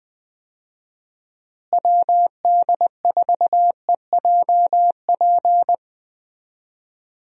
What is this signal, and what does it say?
Morse code, heard as a keyed tone, about 21 wpm, 700 Hz. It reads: WD4EJP
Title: USA Call Signs - 6 Characters - Repeated 3x - 20wpm